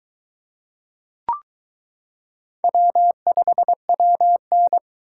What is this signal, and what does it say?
Morse code, heard as a keyed tone, about 22 wpm, 700 Hz. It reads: W5WN